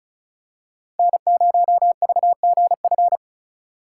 Morse code, heard as a keyed tone, about 34 wpm, 700 Hz. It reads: N0VGF
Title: USA Call Signs - 5 Characters - 6x Character Spacing - 35wpm